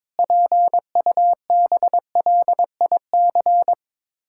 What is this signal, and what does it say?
Morse code, heard as a keyed tone, about 22 wpm, 700 Hz. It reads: PUBLIC
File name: practice / 22wpm / Sets of 7 Words - 22wpm